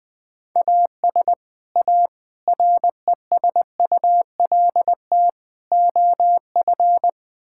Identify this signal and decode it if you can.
Morse code, heard as a keyed tone, about 20 wpm, 700 Hz. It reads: AS A RESULT OF
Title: Build A Phrase - 4-Word Phrases - 20wpm